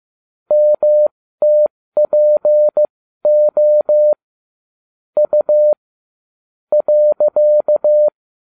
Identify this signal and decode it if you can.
Morse code, heard as a keyed tone, about 15 wpm, spaced as for 13 wpm, 600 Hz. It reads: MTPO U .